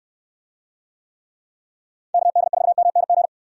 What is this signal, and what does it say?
Morse code, heard as a keyed tone, about 69 wpm, 700 Hz. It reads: KD4MNG